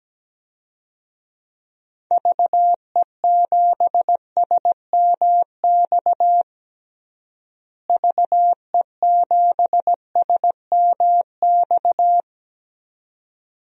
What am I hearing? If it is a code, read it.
Morse code, heard as a keyed tone, about 17 wpm, 700 Hz. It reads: VE7SMX VE7SMX
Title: DXCC-100 Call Signs - 6 Characters - Repeated 2x - 17wpm